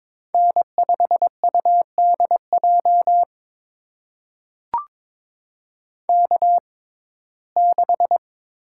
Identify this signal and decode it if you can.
Morse code, heard as a keyed tone, about 22 wpm, spaced as for 16 wpm, 700 Hz. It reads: N5UDJ K6
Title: USA Call Signs - 6x Character Spacing - 22wpm